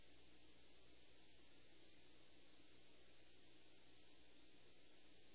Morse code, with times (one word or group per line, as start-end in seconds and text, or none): none